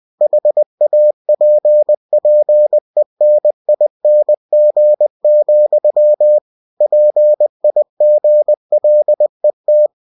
Morse code, heard as a keyed tone, about 20 wpm, 600 Hz. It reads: HAPPENING, PIGLET